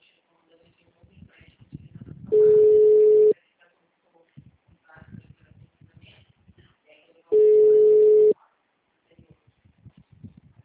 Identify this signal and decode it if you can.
Ringback tone (Europe, 425 Hz), with speech.
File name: out- -2016- - - .14160